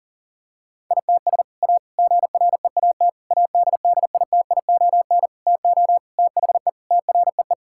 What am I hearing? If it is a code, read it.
Morse code, heard as a keyed tone, about 40 wpm, 700 Hz. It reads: ITS A GREAT ADDITION TO THE TREE
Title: Sentences from Top 1000 Words - Review Part 6 - 40wpm